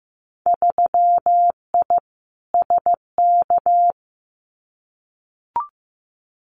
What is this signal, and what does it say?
Morse code, heard as a keyed tone, about 15 wpm, 700 Hz. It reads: 3I SK
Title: Winter Field Day Exchanges Encore 15wpm